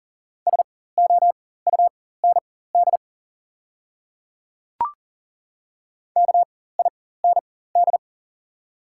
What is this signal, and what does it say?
Morse code, heard as a keyed tone, about 38 wpm, spaced as for 18 wpm, 690 Hz. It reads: SOUND KIND
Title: 4x Character Spacing - Top 300 Words - 40wpm